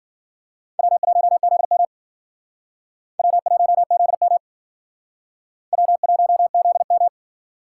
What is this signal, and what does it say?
Morse code, heard as a keyed tone, about 52 wpm, 700 Hz. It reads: W1ZM W1ZM W1ZM